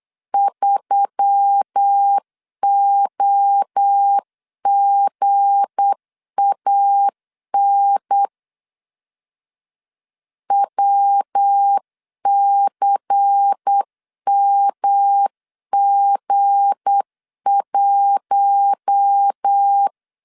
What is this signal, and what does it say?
Morse code, heard as a keyed tone, about 8 wpm, 800 Hz. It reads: 3OGAN WCMG1
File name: LEZIONI / 7 GF9Z / 7 G 8wpm 3118